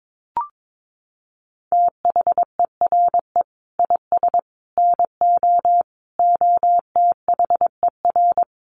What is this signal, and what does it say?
Morse code, heard as a keyed tone, about 22 wpm, 700 Hz. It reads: THERE IS NO OTHER
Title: Sentences from Top 1300 Words 22wpm